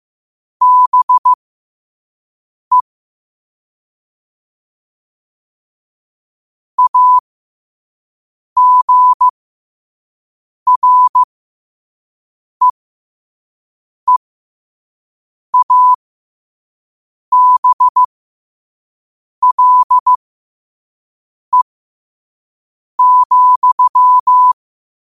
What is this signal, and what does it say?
Morse code, heard as a keyed tone, about 15 wpm, spaced as for 5 wpm, 1000 Hz. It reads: BE AGREEABLE,